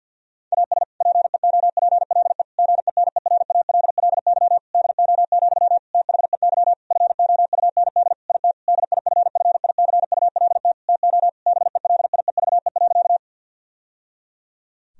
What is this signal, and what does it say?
Morse code, heard as a keyed tone, about 50 wpm, 700 Hz. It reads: AS PEOPLE GENERALLY DO, THEY FOUND IT DIFFICULT TO BELIEVE.
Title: Sentences from Top 1000 Words - Review Part 4 - 50wpm